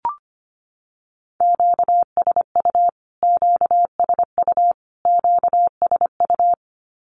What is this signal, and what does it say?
Morse code, heard as a keyed tone, about 25 wpm, 700 Hz. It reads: QSU QSU QSU
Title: Q-Codes Repeated 3x 25wpm